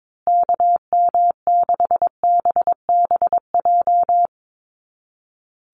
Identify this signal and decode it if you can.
Morse code, heard as a keyed tone, about 22 wpm, 700 Hz. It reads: KM6BBJ
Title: USA Call Signs Repeated 3x 22wpm